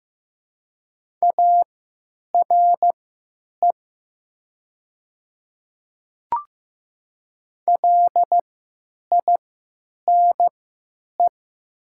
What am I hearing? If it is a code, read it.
Morse code, heard as a keyed tone, about 15 wpm, spaced as for 9 wpm, 700 Hz. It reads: ARE LINE
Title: 3x Character Spacing - Top 200 Words - 15wpm